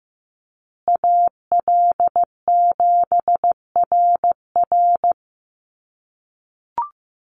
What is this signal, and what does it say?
Morse code, heard as a keyed tone, about 15 wpm, 700 Hz. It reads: AL7RR
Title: USA Call Signs Repeated 3x 15wpm